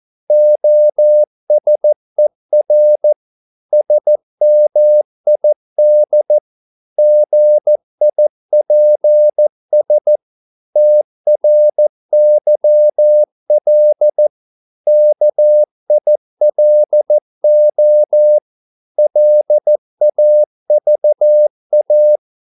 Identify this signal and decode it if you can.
Morse code, heard as a keyed tone, about 14 wpm, 600 Hz. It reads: OSER SMID GIPS TRYL KILO LAVA
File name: Korte ord DK 14wpm